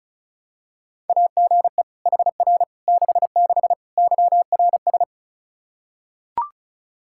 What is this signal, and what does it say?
Morse code, heard as a keyed tone, about 36 wpm, 700 Hz. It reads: AGE HR 66 YRS
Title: Practice QSO - Exchange 3 - More Info - 35wpm